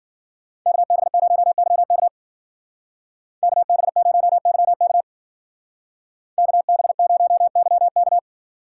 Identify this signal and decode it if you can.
Morse code, heard as a keyed tone, about 49 wpm, 700 Hz. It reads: KB0YK KB0YK KB0YK
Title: Speed Racing - USA 2x2 Call Signs 40wpm